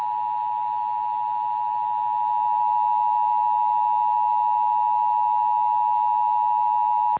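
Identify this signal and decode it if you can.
Emergency Alert System (SAME) tones (the 853 and 960 Hz attention tone). It is heard from 0.5 to 6.4 s.
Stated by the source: Emergency Alert System recording logged from station KBBX.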